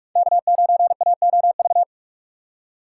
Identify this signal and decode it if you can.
Morse code, heard as a keyed tone, about 46 wpm, 700 Hz. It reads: K9AOV
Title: USA Call Signs - 5 Characters - 3x Character Spacing - 45wpm